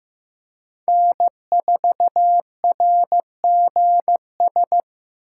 Morse code, heard as a keyed tone, about 15 wpm, 700 Hz. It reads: N4RGS
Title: USA Call Signs - 5 Characters - 4x Character Spacing - 15wpm